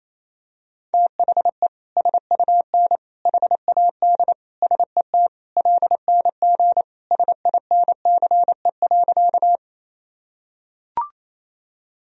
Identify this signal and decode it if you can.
Morse code, heard as a keyed tone, about 28 wpm, 700 Hz. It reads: THE SUN HAD SET LNG SINCE.
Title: Sentences from Top 300 Words and Abbreviations 28wpm